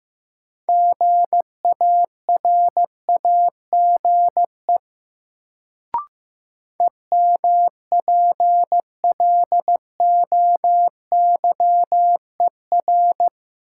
Morse code, heard as a keyed tone, about 15 wpm, 700 Hz. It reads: GARAGE EMPLOYER
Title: Sentences from Top 1600 Words - New Word Drill - 15wpm